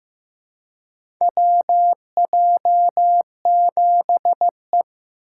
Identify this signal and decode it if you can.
Morse code, heard as a keyed tone, about 15 wpm, 700 Hz. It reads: WJ7E
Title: USA Call Signs - 4 Characters - 4x Character Spacing - 15wpm